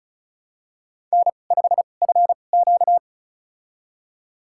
Morse code, heard as a keyed tone, about 36 wpm, spaced as for 25 wpm, 700 Hz. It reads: N5FQ
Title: USA Call Signs - 2x Character Spacing - 35wpm